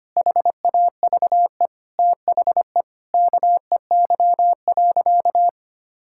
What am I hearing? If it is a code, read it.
Morse code, heard as a keyed tone, about 25 wpm, 700 Hz. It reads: HAVE THE KEY.
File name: practice / 25wpm / Sentences with 4-Word Phrases - 25wpm